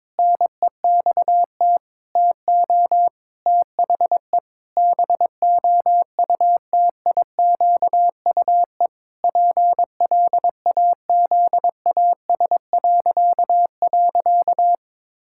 Morse code, heard as a keyed tone, about 22 wpm, 700 Hz. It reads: NEXT TO THE BOUTIQUE PLAZAS..
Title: Pangram Challenge v2 - 22wpm